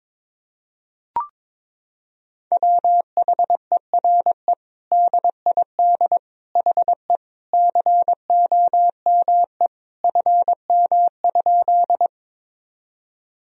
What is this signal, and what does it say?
Morse code, heard as a keyed tone, about 22 wpm, 700 Hz. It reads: WHERE DID HE COME FM?
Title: Sentences from Top 200 Words and Abbreviations 22wpm